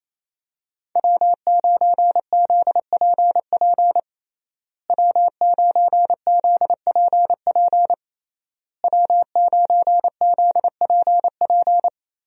Morse code, heard as a keyed tone, about 28 wpm, 700 Hz. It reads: W9ZPP W9ZPP W9ZPP
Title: USA Call Signs - 5 Characters - Repeated 3x - 28wpm